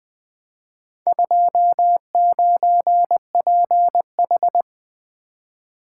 Morse code, heard as a keyed tone, about 20 wpm, 700 Hz. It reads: 29PH
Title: Four Letter-Number Combinations - 20wpm